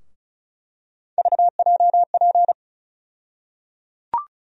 Morse code, heard as a keyed tone, about 34 wpm, 700 Hz. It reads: VJP
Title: USA Call Sign Postfixes - 3 Letters 35wpm